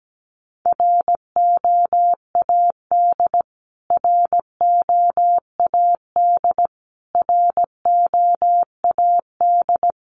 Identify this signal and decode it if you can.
Morse code, heard as a keyed tone, about 17 wpm, 700 Hz. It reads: ROAD ROAD ROAD
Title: Top 500 Words Repeated 17wpm